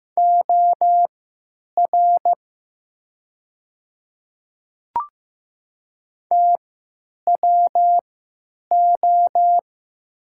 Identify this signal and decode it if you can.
Morse code, heard as a keyed tone, about 15 wpm, spaced as for 9 wpm, 700 Hz. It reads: OR TWO
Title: 3x Character Spacing - Top 100 Words - 15wpm